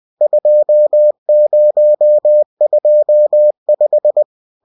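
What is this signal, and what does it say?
Morse code, heard as a keyed tone, about 20 wpm, 600 Hz. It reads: 2025
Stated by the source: Bulletin sent in CW.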